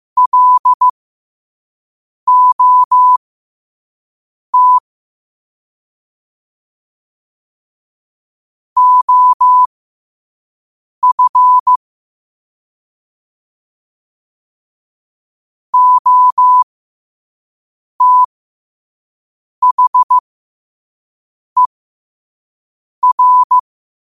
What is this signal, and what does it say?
Morse code, heard as a keyed tone, about 15 wpm, spaced as for 5 wpm, 1000 Hz. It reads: LOT OF OTHER
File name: qotdfiles / 5wpm